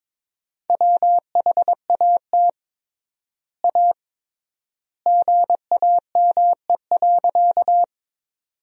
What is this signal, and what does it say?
Morse code, heard as a keyed tone, about 21 wpm, 700 Hz. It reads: WHAT A GAME.
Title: 3x Word-Spacing - Sentences from Top 500 Words - 22wpm